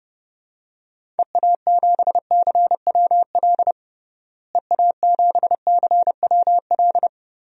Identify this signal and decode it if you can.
Morse code, heard as a keyed tone, about 30 wpm, 700 Hz. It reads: EA7CWL EA7CWL
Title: DXCC-100 Call Signs - 6 Characters - Repeated 3x - 30wpm